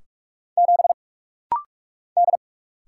Morse code, heard as a keyed tone, about 46 wpm, 700 Hz. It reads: 7 D
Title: Single Letter-Number - Rapid-Fire 45wpm